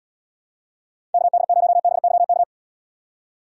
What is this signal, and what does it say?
Morse code, heard as a keyed tone, about 71 wpm, 700 Hz. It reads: KD0BYK